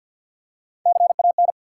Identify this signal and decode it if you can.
Morse code, heard as a keyed tone, about 54 wpm, 700 Hz. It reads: CAN
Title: Three Letter Words - 50wpm